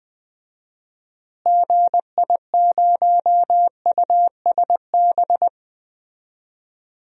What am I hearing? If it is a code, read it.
Morse code, heard as a keyed tone, about 20 wpm, 700 Hz. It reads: GI0USB
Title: DXCC-100 Call Signs - 6 Characters - Repeated 3x - 20wpm